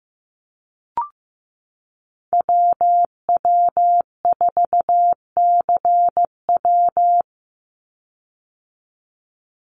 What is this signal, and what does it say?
Morse code, heard as a keyed tone, about 15 wpm, 700 Hz. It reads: WW4CW
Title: USA Call Signs 15wpm